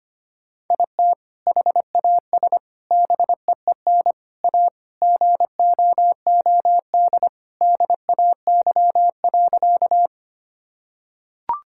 Morse code, heard as a keyed tone, about 25 wpm, 700 Hz. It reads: IT HAS BEEN A GOOD DAY.
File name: practice / 25wpm / Sentences with 2-Word Phrases - 25wpm